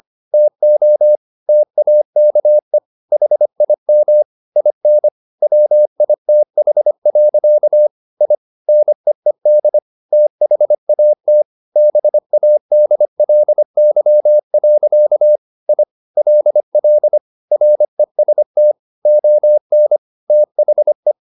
Morse code, heard as a keyed tone, about 25 wpm, 600 Hz. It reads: TO TAKE HIM IN WITH. I NEED THAT BADLY. I LL REST ON THE